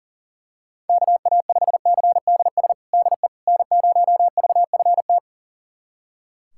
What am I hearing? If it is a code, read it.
Morse code, heard as a keyed tone, about 40 wpm, 700 Hz. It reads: KA5CDS DE N0VFT
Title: USA Callsign DE Callsign 40wpm